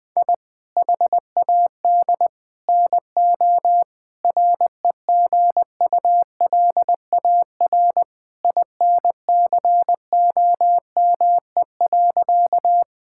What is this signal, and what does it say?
Morse code, heard as a keyed tone, about 20 wpm, 700 Hz. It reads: I HAD NO REGULAR INCOME.
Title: Sentences from Top 1000 Words - Review Part 6 - 20wpm